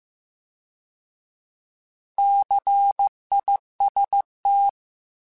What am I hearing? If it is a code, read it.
Morse code, heard as a keyed tone, about 15 wpm, 800 Hz. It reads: CIST